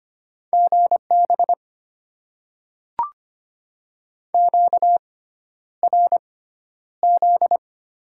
Morse code, heard as a keyed tone, about 25 wpm, spaced as for 11 wpm, 690 Hz. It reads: GB QRZ
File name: Top 90 QSO Elements - 6x Character Spacing - 25wpm